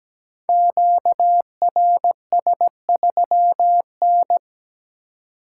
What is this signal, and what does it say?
Morse code, heard as a keyed tone, about 17 wpm, 700 Hz. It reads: QRS3N
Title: Code Groups - 5 Letter:Numbers - 17wpm